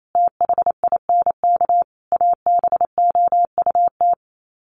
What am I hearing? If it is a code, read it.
Morse code, heard as a keyed tone, about 28 wpm, 700 Hz. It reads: THINK ABOUT
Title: Sets of 5 Words 28wpm